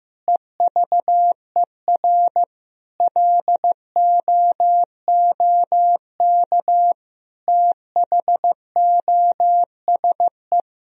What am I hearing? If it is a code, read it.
Morse code, heard as a keyed tone, about 16 wpm, 700 Hz. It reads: EVER LOOK THOSE
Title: Sentences from Top 1000 Words - Review Part 2 - 15wpm